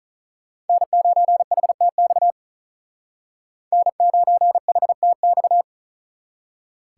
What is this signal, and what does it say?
Morse code, heard as a keyed tone, about 43 wpm, 700 Hz. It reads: N9HTX N9HTX
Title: Speed Racing - USA 1x3 Call Signs 35wpm